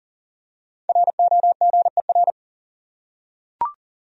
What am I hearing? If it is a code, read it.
Morse code, heard as a keyed tone, about 40 wpm, 700 Hz. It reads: ROGER
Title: USA Common Names - Max 5 Letters - 40wpm